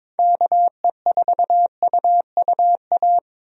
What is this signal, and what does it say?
Morse code, heard as a keyed tone, about 22 wpm, 700 Hz. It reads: KE4UUA
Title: USA Call Signs - 6 Characters - Repeated 2x - 22wpm